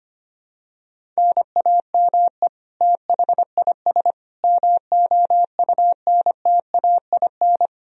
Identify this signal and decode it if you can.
Morse code, heard as a keyed tone, about 25 wpm, 700 Hz. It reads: NAME THIS MOUNTAIN
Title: Sentences from Top 1000 Words - Review Part 4 - 25wpm